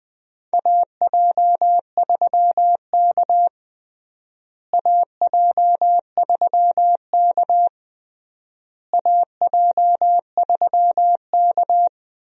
Morse code, heard as a keyed tone, about 20 wpm, 700 Hz. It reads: AJ3K AJ3K AJ3K
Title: USA Call Signs - 4 Characters - Repeated 3x - 20wpm